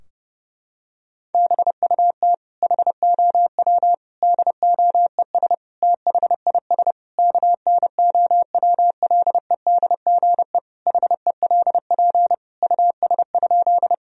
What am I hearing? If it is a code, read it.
Morse code, heard as a keyed tone, about 30 wpm, 700 Hz. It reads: BUT HOW DOES THIS KNOWLEDGE HELP US?
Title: Sentences from Top 1000 Words - Review Part 3 - 30wpm